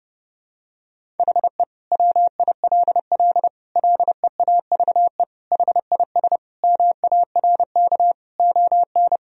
Morse code, heard as a keyed tone, about 30 wpm, 700 Hz. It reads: HE WILL LEAVE HIS MARK ON
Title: Sentences from Top 1000 Words - Review Part 1 - 30wpm